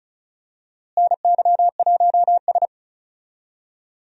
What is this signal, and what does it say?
Morse code, heard as a keyed tone, about 36 wpm, 700 Hz. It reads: NY1S
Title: USA Call Signs - 4 Characters - Repeated 2x - 35wpm